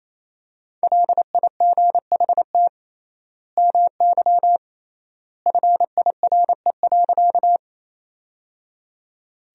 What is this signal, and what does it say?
Morse code, heard as a keyed tone, about 28 wpm, 700 Hz. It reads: LIGHT MY FIRE.
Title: 3x Word-Spacing - Sentences from Top 400 Words - 28wpm